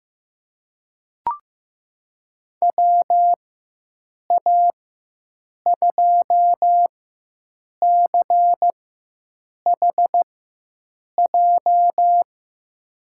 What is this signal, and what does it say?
Morse code, heard as a keyed tone, about 15 wpm, spaced as for 7 wpm, 700 Hz. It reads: WA2CHJ
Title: USA Call Signs - 6 Characters - 4x Character Spacing - 15wpm